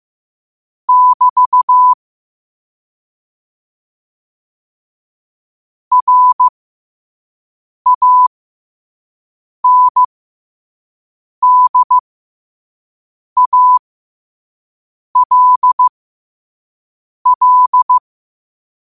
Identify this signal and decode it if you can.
Morse code, heard as a keyed tone, about 15 wpm, spaced as for 5 wpm, 1000 Hz. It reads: <BT> RANDALL
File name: qotdfiles / 5wpm